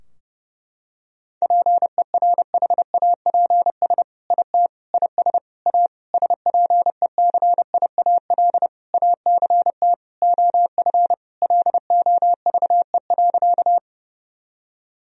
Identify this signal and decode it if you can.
Morse code, heard as a keyed tone, about 32 wpm, 700 Hz. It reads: PERHAPS IT IS A SPECIAL ACT OF LOVE.
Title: Sentences from Top 1000 Words - Review Part 3 - 30wpm